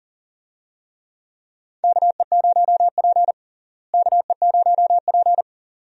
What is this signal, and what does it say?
Morse code, heard as a keyed tone, about 38 wpm, 700 Hz. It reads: KE0P KE0P
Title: USA Call Signs - 4 Characters - Repeated 2x - 40wpm